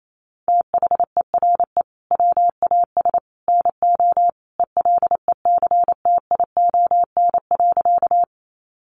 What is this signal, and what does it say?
Morse code, heard as a keyed tone, about 28 wpm, 700 Hz. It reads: THERE WAS NO ELECTION.
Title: Sentences from Top 1500 Words 28wpm